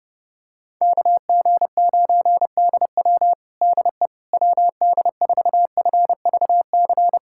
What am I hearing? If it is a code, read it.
Morse code, heard as a keyed tone, about 30 wpm, 700 Hz. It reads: KG9DW DE WD4FVC
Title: USA Callsign DE Callsign 30wpm